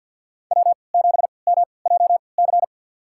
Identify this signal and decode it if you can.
Morse code, heard as a keyed tone, about 47 wpm, spaced as for 27 wpm, 700 Hz. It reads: W7MJC
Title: USA Call Signs - 5 Characters - 3x Character Spacing - 50wpm